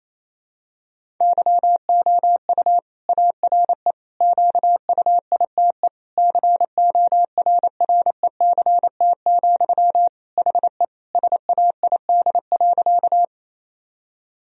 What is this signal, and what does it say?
Morse code, heard as a keyed tone, about 28 wpm, 700 Hz. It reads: YOU ARE QUITE CORRECT, HE SAID.
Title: Sentences from Top 1000 Words - Review Part 2 - 28wpm